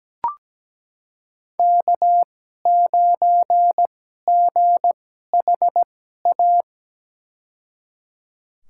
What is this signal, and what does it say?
Morse code, heard as a keyed tone, about 17 wpm, spaced as for 12 wpm, 700 Hz. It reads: K9GHA
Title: USA Call Signs - 5 Characters - 2x Character Spacing - 17wpm